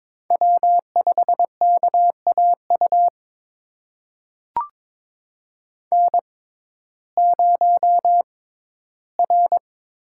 Morse code, heard as a keyed tone, about 22 wpm, spaced as for 14 wpm, 700 Hz. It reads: W5KAU N0R
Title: USA Call Signs - 5 Characters - 6x Character Spacing - 22wpm